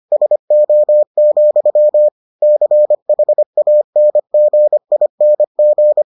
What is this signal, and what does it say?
Morse code, heard as a keyed tone, about 25 wpm, 600 Hz. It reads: SO, CHANGING